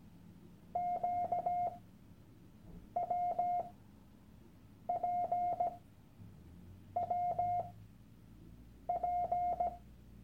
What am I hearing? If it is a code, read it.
Morse code, heard as a keyed tone, about 17 wpm, spaced as for 6 wpm, 710 Hz. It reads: QWPWP
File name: Workshop Dateien / CW-Online-Kurs / Lektion 001